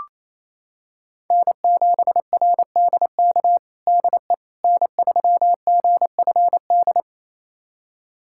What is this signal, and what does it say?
Morse code, heard as a keyed tone, about 28 wpm, 700 Hz. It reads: N7RDK DE N3GFD
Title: USA Callsign DE Callsign - 5 Characters - 28wpm